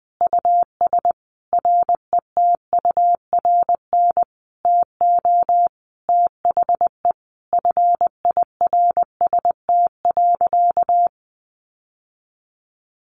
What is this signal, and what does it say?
Morse code, heard as a keyed tone, about 20 wpm, 700 Hz. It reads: US RETURN TO THE FIRST.
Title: Sentences from Top 800 Words 20wpm